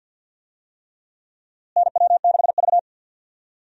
Morse code, heard as a keyed tone, about 52 wpm, 700 Hz. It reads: NW6V
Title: USA Call Signs - 4 Characters - Repeated 3x - 50wpm